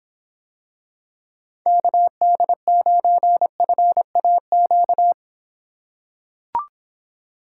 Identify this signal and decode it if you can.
Morse code, heard as a keyed tone, about 26 wpm, 700 Hz. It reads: KD9FAQ